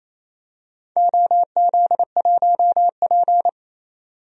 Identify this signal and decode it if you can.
Morse code, heard as a keyed tone, about 28 wpm, 700 Hz. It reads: OZ1P
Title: DXCC-100 Call Signs - 4 Characters - 28wpm